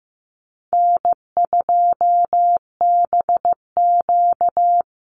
Morse code, heard as a keyed tone, about 15 wpm, 700 Hz. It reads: N2BQ
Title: USA Call Signs 15wpm